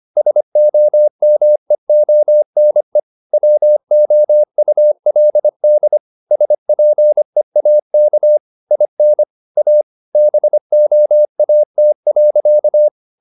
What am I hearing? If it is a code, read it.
Morse code, heard as a keyed tone, about 25 wpm, 600 Hz. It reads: SOMEONE WOULD SPEAK IN A BOAT.
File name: the old man and the sea part 1 25wpm